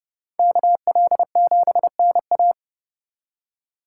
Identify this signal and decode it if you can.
Morse code, heard as a keyed tone, about 30 wpm, 700 Hz. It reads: KL7NA